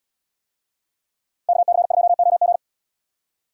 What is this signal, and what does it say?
Morse code, heard as a keyed tone, about 70 wpm, 700 Hz. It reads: KC2OG